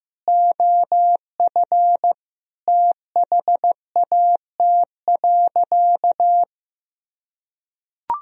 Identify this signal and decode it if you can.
Morse code, heard as a keyed tone, about 15 wpm, 700 Hz. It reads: OF THAT.
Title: Sentences from Top 1000 Words - Review Part 2 - 15wpm